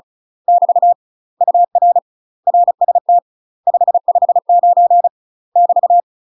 Morse code, heard as a keyed tone, about 36 wpm, 700 Hz. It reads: <BT> UR RST 559 <BT>
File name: UK MASTER QSOs TRAINING / 35WPM / UK MASTER QSOs 0041